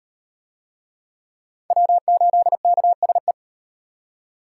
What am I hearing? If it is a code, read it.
Morse code, heard as a keyed tone, about 33 wpm, 700 Hz. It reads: W8KSE